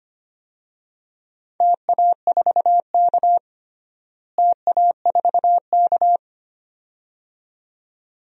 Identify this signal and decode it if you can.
Morse code, heard as a keyed tone, about 25 wpm, 700 Hz. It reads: TA4K TA4K
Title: DXCC-100 Call Signs - 4 Characters - Repeated 2x - 25wpm